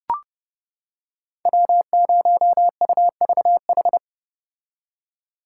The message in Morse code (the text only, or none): W0UVH